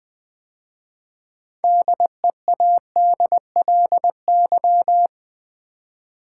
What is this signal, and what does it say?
Morse code, heard as a keyed tone, about 20 wpm, 700 Hz. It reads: DEADLY
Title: 6 Letter Words - First 3 Letters Given - 20wpm